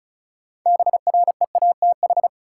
Morse code, heard as a keyed tone, about 36 wpm, 700 Hz. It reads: BREATH
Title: Sets of 5 Words Encore - New Word Drill - 35wpm